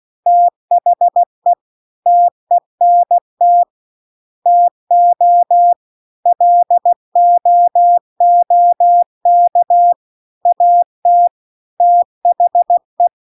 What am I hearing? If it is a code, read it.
Morse code, heard as a keyed tone, about 16 wpm, 700 Hz. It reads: THE TENT TO LOOK AT THE
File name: 16WPM